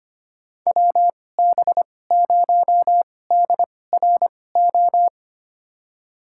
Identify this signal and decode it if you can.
Morse code, heard as a keyed tone, about 25 wpm, spaced as for 18 wpm, 700 Hz. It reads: WB0DRO
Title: USA Call Signs - 6 Characters - 2x Character Spacing - 25wpm